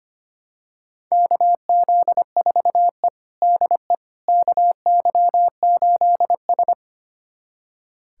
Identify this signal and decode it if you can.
Morse code, heard as a keyed tone, about 25 wpm, 700 Hz. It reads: KZ4E DE KY8S
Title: USA Callsign DE Callsign - 4 Characters - 25wpm